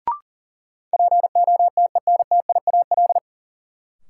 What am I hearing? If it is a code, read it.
Morse code, heard as a keyed tone, about 40 wpm, 700 Hz. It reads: POTENTIAL
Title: Sentences from Top 1000 Words - Review Part 4 - New Word Drill - 40wpm